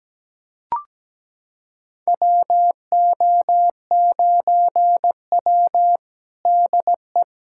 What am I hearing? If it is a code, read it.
Morse code, heard as a keyed tone, about 17 wpm, 700 Hz. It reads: WO9W DE
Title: USA Callsign DE Callsign - 4 Characters - 17wpm